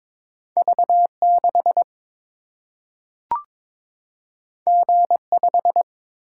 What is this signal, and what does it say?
Morse code, heard as a keyed tone, about 22 wpm, 700 Hz. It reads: V6 G5
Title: Single-Letter Single-Number 22wpm